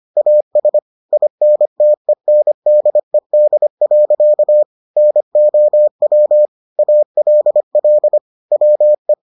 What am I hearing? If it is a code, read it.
Morse code, heard as a keyed tone, about 25 wpm, 600 Hz. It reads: AS INTENDED. NOW ALL WE